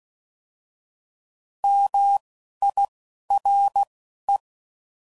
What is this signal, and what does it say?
Morse code, heard as a keyed tone, about 16 wpm, spaced as for 11 wpm, 800 Hz. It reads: MIRE